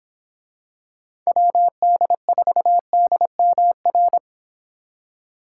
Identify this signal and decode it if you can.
Morse code, heard as a keyed tone, about 25 wpm, 700 Hz. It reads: WD4DMR
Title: Speed Racing - USA 2x3 Call Signs 22wpm